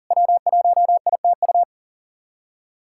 Morse code, heard as a keyed tone, about 40 wpm, 690 Hz. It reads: W1ITU
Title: SOTA QSO Practice 40wpm